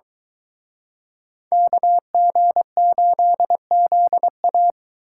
Morse code, heard as a keyed tone, about 23 wpm, 700 Hz. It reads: KG8ZA